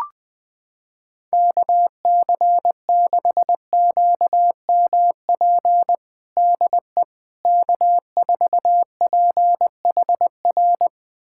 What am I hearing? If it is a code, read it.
Morse code, heard as a keyed tone, about 21 wpm, 700 Hz. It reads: KC6QMP DE K4PHR
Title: USA Callsign DE Callsign 20wpm